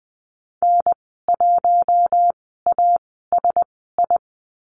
Morse code, heard as a keyed tone, about 20 wpm, spaced as for 14 wpm, 700 Hz. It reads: N1ASI
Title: 088 Lesson 15g - Number 5 - Call Signs 20wpm